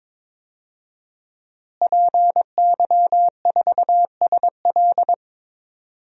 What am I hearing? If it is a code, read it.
Morse code, heard as a keyed tone, about 22 wpm, 700 Hz. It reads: PY4SL
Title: DXCC-100 Call Signs - 5 Characters - Repeated 3x - 22wpm